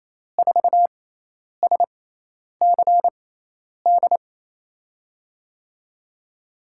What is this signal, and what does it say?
Morse code, heard as a keyed tone, about 28 wpm, spaced as for 10 wpm, 700 Hz. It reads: 4SCD
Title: USA Call Signs - 6x Character Spacing - 28wpm